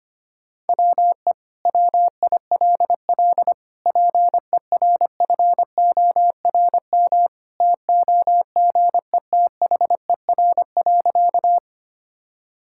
Morse code, heard as a keyed tone, about 25 wpm, 700 Hz. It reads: WE WILL PERFORM TOGETHER.
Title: Sentences from Top 1000 Words - Review Part 5 - 25wpm